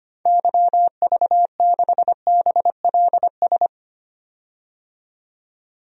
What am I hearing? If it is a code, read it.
Morse code, heard as a keyed tone, about 25 wpm, 700 Hz. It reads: YV6BLS
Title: DXCC-100 Call Signs - 6 Characters - 25wpm